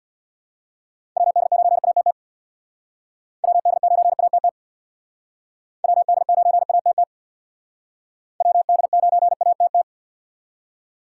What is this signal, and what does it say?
Morse code, heard as a keyed tone, about 62 wpm, 700 Hz. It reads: WD9ATT WD9ATT WD9ATT WD9ATT
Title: Speed Racing - USA 2x3 Call Signs 50wpm